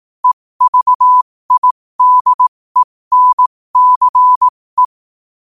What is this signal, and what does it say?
Morse code, heard as a keyed tone, about 18 wpm, spaced as for 15 wpm, 1000 Hz. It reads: EVIDENCE
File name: qotdfiles / 15wpm